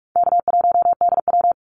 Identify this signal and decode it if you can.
Morse code, heard as a keyed tone, about 44 wpm, 700 Hz. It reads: K1DW
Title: Easy Call Signs - 45wpm